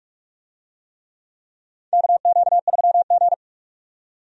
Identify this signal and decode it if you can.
Morse code, heard as a keyed tone, about 46 wpm, 700 Hz. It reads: KQ3G
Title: USA Call Signs - 4 Characters - 45wpm